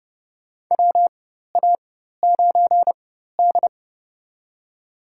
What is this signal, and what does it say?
Morse code, heard as a keyed tone, about 32 wpm, spaced as for 14 wpm, 700 Hz. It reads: WA9D
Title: USA Call Signs - 4 Characters - 4x Character Spacing - 30wpm